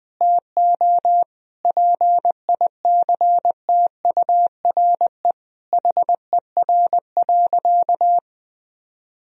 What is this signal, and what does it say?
Morse code, heard as a keyed tone, about 20 wpm, 700 Hz. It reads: TO PICTURE HER.
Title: Sentences from Top 1000 Words - Review Part 1 - 20wpm